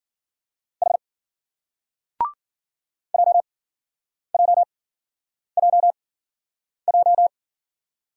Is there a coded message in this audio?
Yes: Morse code from 0.8 to 7.3 s, heard as a keyed tone, about 50 wpm, spaced as for 9 wpm, 700 Hz. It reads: H JJJJ